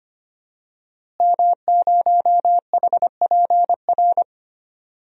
Morse code, heard as a keyed tone, about 25 wpm, 700 Hz. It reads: M0HPR
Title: DXCC-100 Call Signs - 5 Characters - Repeated 2x - 25wpm